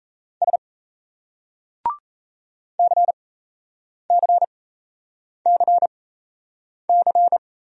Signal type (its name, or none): Morse code